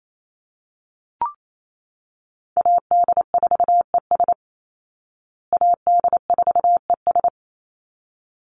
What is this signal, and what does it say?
Morse code, heard as a keyed tone, about 28 wpm, 700 Hz. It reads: AD4ES AD4ES
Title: Call Signs Repeated 3x - 28wpm